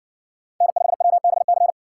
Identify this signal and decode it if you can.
Morse code, heard as a keyed tone, about 60 wpm, 700 Hz. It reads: N5WBC